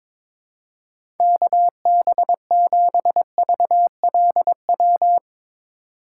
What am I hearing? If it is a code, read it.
Morse code, heard as a keyed tone, about 21 wpm, 700 Hz. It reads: KB7VLW
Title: USA Call Signs - 6 Characters - 2x Character Spacing - 22wpm